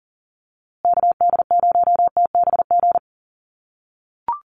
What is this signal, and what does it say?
Morse code, heard as a keyed tone, about 40 wpm, 700 Hz. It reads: KD0TBG
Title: USA Call Signs Repeated 3x 40wpm